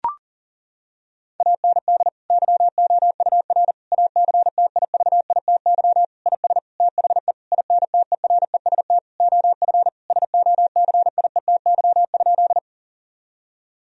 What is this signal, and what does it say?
Morse code, heard as a keyed tone, about 40 wpm, 700 Hz. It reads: AND YOUR ACTIVITY IS THE INTEREST OF SOCIETY?
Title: Sentences from Top 1000 Words - Review Part 4 - 40wpm